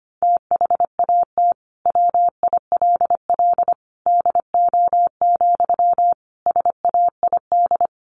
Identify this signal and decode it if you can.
Morse code, heard as a keyed tone, about 25 wpm, 700 Hz. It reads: THAT WILL DO, SAID
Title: Sentences from Top 100 Words 25wpm